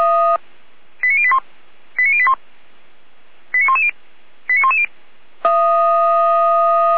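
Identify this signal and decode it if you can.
Five-tone selective call (ZVEI), heard as quick runs of five tones, tones of about 70 ms. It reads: 89081 89081 89110 89110
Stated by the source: ZVEI five-tone selective call 89401 received by SDR.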